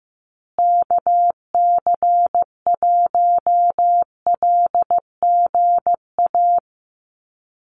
Morse code, heard as a keyed tone, about 15 wpm, 700 Hz. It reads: KC1LGA
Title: USA Call Signs Repeated 3x 15wpm